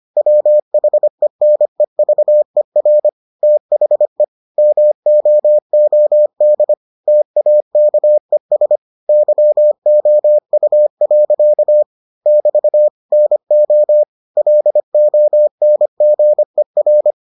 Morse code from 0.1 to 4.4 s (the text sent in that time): WHENEVER THE